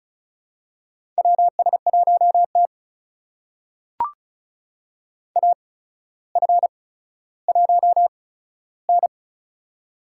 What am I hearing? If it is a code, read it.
Morse code, heard as a keyed tone, about 36 wpm, spaced as for 15 wpm, 700 Hz. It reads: WS1T AF1N
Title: USA Call Signs - 4 Characters - 8x Character Spacing - 35wpm